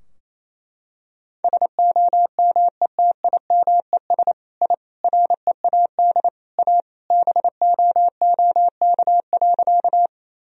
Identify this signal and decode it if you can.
Morse code, heard as a keyed tone, about 28 wpm, 700 Hz. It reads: SOMETIMES I READ A BOOK.
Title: Sentences from Top 1000 Words - Review Part 3 - 28wpm